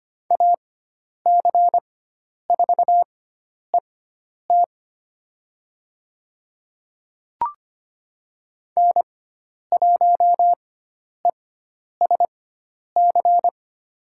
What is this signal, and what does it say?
Morse code, heard as a keyed tone, about 25 wpm, spaced as for 10 wpm, 700 Hz. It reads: AC4ET N1ESC
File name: Lesson 16g - Letter c - Call Signs 25wpm